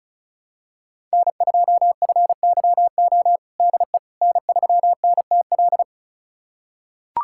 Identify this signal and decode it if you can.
Morse code, heard as a keyed tone, about 36 wpm, 700 Hz. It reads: N2FYO DE N3NTL
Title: USA Callsign DE Callsign - 5 Characters - 35wpm